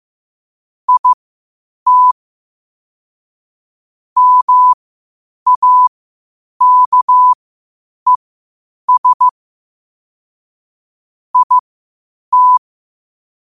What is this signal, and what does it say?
Morse code, heard as a keyed tone, about 15 wpm, spaced as for 8 wpm, 1000 Hz. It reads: IT MAKES IT
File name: qotdfiles / 8wpm